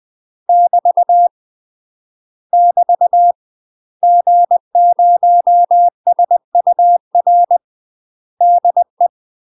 Morse code, heard as a keyed tone, about 20 wpm, 700 Hz. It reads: <BT> <BT> G0SUR DE
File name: UK MASTER QSOs TRAINING / 20WPM / UK MASTER QSOs 0012